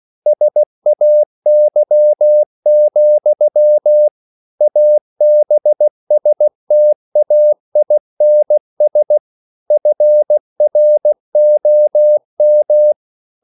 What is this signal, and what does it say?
Morse code, heard as a keyed tone, about 16 wpm, 600 Hz. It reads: SAY, ABSTAINS FROM
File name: CW / Practice / EnglishQuotations0216